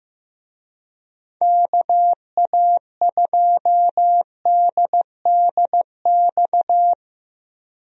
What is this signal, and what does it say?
Morse code, heard as a keyed tone, about 15 wpm, 700 Hz. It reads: KA2DDX
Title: USA Call Signs - 6 Characters - Repeated 2x - 15wpm